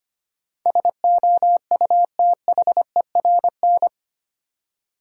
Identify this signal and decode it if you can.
Morse code, heard as a keyed tone, about 25 wpm, 700 Hz. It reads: SOUTHERN
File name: Eight Letter Words - 25wpm